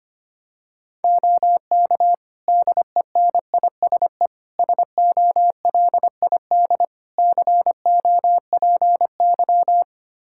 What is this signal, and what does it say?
Morse code, heard as a keyed tone, about 25 wpm, 700 Hz. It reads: OK DENISE SOLID COPY
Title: K1USN SST QSO Practice 25wpm